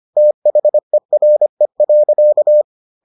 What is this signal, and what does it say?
Morse code, heard as a keyed tone, about 25 wpm, 600 Hz. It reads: THERE.